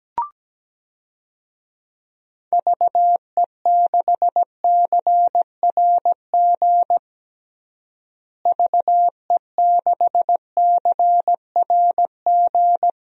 Morse code, heard as a keyed tone, about 17 wpm, 700 Hz. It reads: VE6CRG VE6CRG
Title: DXCC-100 Call Signs - 6 Characters - Repeated 3x - 17wpm